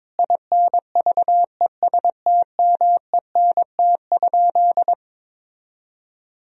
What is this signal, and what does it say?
Morse code, heard as a keyed tone, about 22 wpm, 700 Hz. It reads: INVESTMENT?
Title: Sentences from Top 1000 Words - Review Part 4 - 22wpm